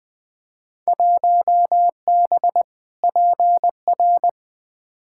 Morse code, heard as a keyed tone, about 20 wpm, 700 Hz. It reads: 1B PR
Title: Field Day Exchanges Encore - 20wpm